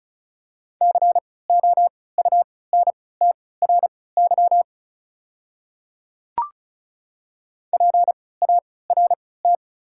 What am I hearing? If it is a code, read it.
Morse code, heard as a keyed tone, about 34 wpm, spaced as for 20 wpm, 700 Hz. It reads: COUNTRY PART
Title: 3x Character Spacing - Top 300 Words - 35wpm